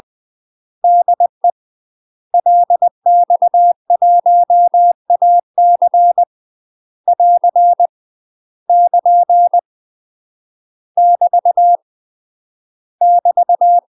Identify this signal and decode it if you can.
Morse code, heard as a keyed tone, about 20 wpm, 700 Hz. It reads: DE LX1AC <AR> <KN> <BT> <BT>
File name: UK MASTER QSOs TRAINING / 20WPM / UK MASTER QSOs 0036